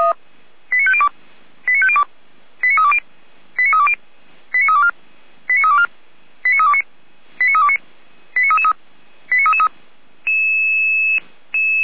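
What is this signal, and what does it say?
Five-tone selective call (ZVEI), heard as quick runs of five tones, tones of about 70 ms. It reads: 89502 89502 89320 89320 89324 89324 89329 89329 89333
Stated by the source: ZVEI five-tone selective call 89202, received with an SDR.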